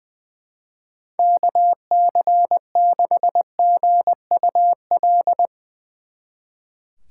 Morse code, heard as a keyed tone, about 22 wpm, 700 Hz. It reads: KC6GUL
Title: Speed Racing - USA Call Signs 20wpm